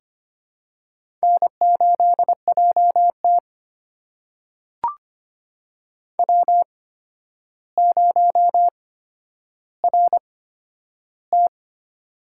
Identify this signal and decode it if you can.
Morse code, heard as a keyed tone, about 25 wpm, spaced as for 11 wpm, 700 Hz. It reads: N8JT W0RT
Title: USA Call Signs - 4 Characters - 8x Character Spacing - 25wpm